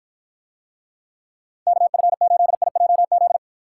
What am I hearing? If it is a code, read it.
Morse code, heard as a keyed tone, about 53 wpm, 700 Hz. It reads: KF8IJZ